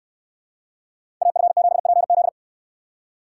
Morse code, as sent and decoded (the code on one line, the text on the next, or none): .- ..-. --... .--. --..
AF7PZ